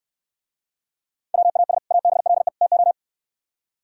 Morse code, heard as a keyed tone, about 67 wpm, 700 Hz. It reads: WAS ABLE TO